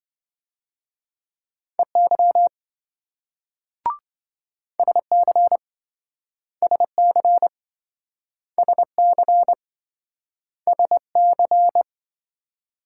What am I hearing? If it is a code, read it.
Morse code, heard as a keyed tone, about 27 wpm, 700 Hz. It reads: EY SC SC SC SC